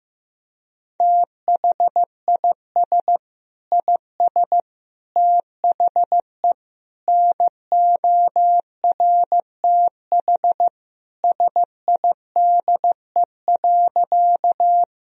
Morse code, heard as a keyed tone, about 15 wpm, 700 Hz. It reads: THIS IS THE NORTH SIDE.
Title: Sentences from Top 1000 Words - Review Part 1 - 15wpm